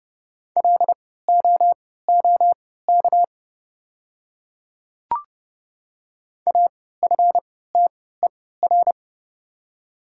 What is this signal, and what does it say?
Morse code, heard as a keyed tone, about 30 wpm, spaced as for 17 wpm, 700 Hz. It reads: LOOK AFTER
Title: 3x Character Spacing - Top 500 Words - 30wpm